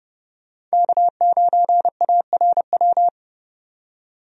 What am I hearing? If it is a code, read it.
Morse code, heard as a keyed tone, about 32 wpm, 700 Hz. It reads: K9ARW
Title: USA Call Signs - 5 Characters - 30wpm